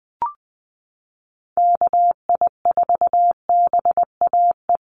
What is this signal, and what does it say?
Morse code, heard as a keyed tone, about 19 wpm, 700 Hz. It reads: KI4BAE
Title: USA Call Signs Repeated 2x 20wpm